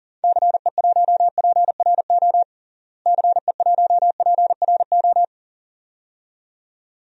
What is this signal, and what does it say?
Morse code, heard as a keyed tone, about 40 wpm, 700 Hz. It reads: CE1PRO CE1PRO
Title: DXCC-100 Call Signs - 6 Characters - Repeated 2x - 40wpm